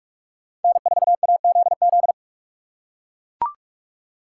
Morse code, heard as a keyed tone, about 39 wpm, 700 Hz. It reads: N4AZZ